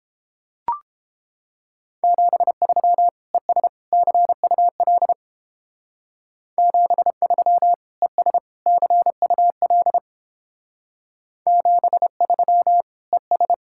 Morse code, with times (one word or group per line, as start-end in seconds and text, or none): 2.0-3.1 73
3.3-3.7 ES
3.9-5.1 CUL
6.6-7.7 73
8.0-8.4 ES
8.7-10.0 CUL
11.5-12.8 73
13.1-13.5 ES